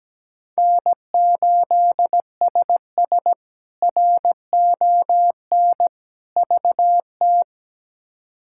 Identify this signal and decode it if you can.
Morse code, heard as a keyed tone, about 17 wpm, 700 Hz. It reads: N8SS RON VT
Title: K1USN SST QSO Practice - 17wpm